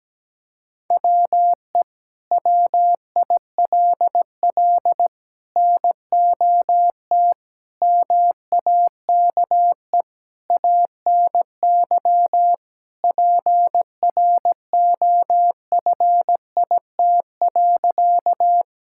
Morse code, heard as a keyed tone, about 17 wpm, 700 Hz. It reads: WE WILL NOT MAKE ANY PROFIT.
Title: Sentences from Top 1000 Words - Review Part 5 - 17wpm